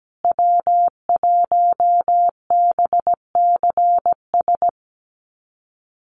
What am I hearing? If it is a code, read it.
Morse code, heard as a keyed tone, about 17 wpm, 700 Hz. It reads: W1BCS
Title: USA Call Signs Repeated 3x 17wpm